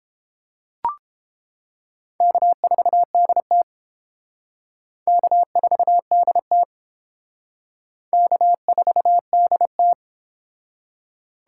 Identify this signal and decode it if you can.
Morse code, heard as a keyed tone, about 29 wpm, 700 Hz. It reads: K4DT K4DT K4DT